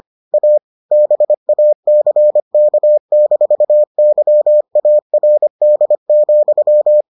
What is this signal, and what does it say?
Morse code, heard as a keyed tone, about 25 wpm, 600 Hz. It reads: A BACK-YARD,